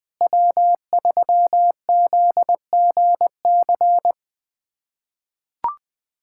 Morse code, heard as a keyed tone, about 20 wpm, 700 Hz. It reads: W3ZGC
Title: USA Callsign DE Callsign 20wpm